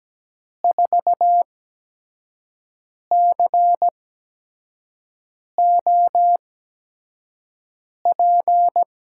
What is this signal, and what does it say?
Morse code, heard as a keyed tone, about 17 wpm, spaced as for 5 wpm, 700 Hz. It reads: 4COP
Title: USA Call Signs - 8x Character Spacing - 17wpm